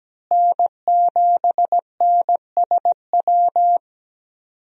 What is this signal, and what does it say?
Morse code, heard as a keyed tone, about 17 wpm, 700 Hz. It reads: N7NSW
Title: USA Call Signs - 5 Characters - Repeated 2x - 17wpm